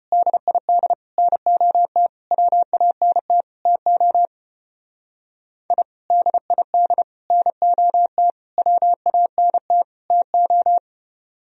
Morse code, heard as a keyed tone, about 33 wpm, 700 Hz. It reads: DID NOT WANT TO I DID NOT WANT TO